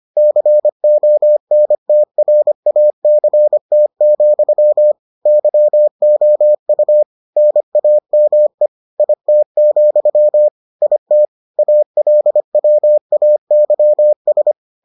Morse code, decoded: CONTRACT, YOU NAME IT, IT ALWAYS

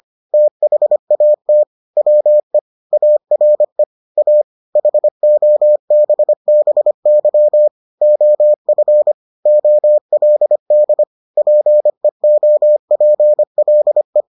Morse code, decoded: THAT WE ARE A HOBBY OF OLD PEOPLE

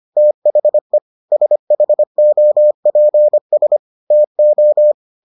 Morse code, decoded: THE SHOPS TO

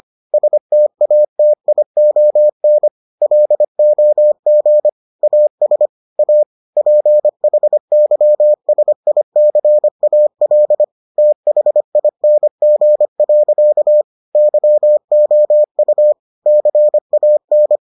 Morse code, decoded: STATION LOG AS A PHYSICAL THING. YOU CAN

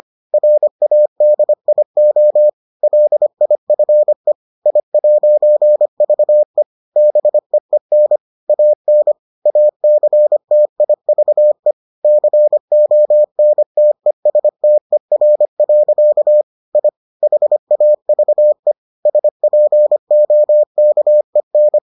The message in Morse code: RADIO LIFE I'VE BEEN AN ACTIVE CONTESTER. I HAVE SPOKEN

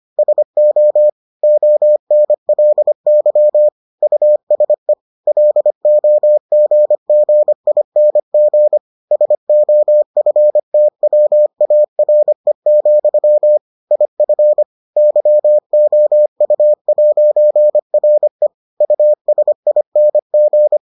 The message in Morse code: SO ONLY USE LOGGING SOFTWARE, IF YOU'RE USING

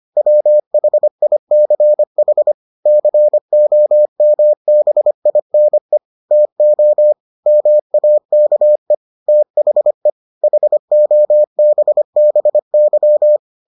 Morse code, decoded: WHICH COMBINE TO MAKE THE HOBBY

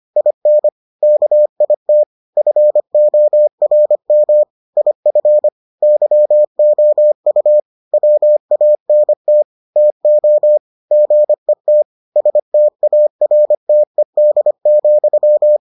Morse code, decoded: IN KIT FORM IF YOU WANT TO GET STARTED,

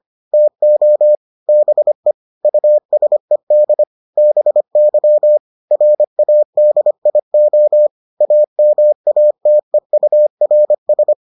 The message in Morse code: TO BE USED BY RADIO AMATEURS

